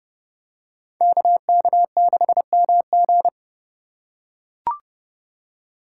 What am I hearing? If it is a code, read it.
Morse code, heard as a keyed tone, about 31 wpm, 700 Hz. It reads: KK6MG
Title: USA Call Signs - 5 Characters - 4x Character Spacing - 30wpm